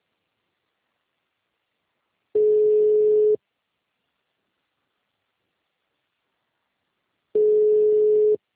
Ringback tone (Europe, 425 Hz).